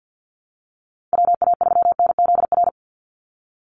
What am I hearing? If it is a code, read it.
Morse code, heard as a keyed tone, about 50 wpm, 700 Hz. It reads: WA3NZR
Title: Call Signs Repeated 3x - 50wpm